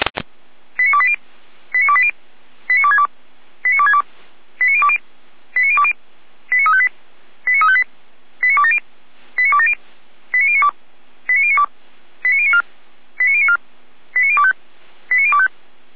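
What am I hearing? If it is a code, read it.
Five-tone selective call (ZVEI), heard as quick runs of five tones, tones of about 70 ms. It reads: 89280 89280 89262 89262 89020 89020 89358 89358 89270 89270 89092 89092 89094 89094 89025 89025
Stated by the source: ZVEI five-tone call 89330 received by SDR.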